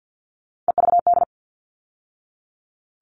Morse code, heard as a keyed tone, about 48 wpm, 700 Hz. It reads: EVD
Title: All Cut Numbers - 3-Digit 50wpm